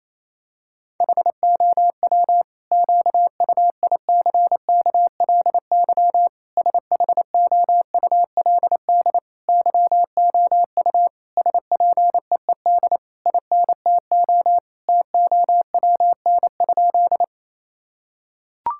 Morse code, heard as a keyed tone, about 28 wpm, 700 Hz. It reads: HOW QUICKLY SHOULD YOU SPEED INTO TOWN?
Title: Sentences from Top 1000 Words - Review Part 5 - 28wpm